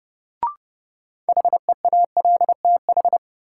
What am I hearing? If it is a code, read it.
Morse code, heard as a keyed tone, about 32 wpm, 700 Hz. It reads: HEALTH
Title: Sets of 5 Words - New Word Drill - 30wpm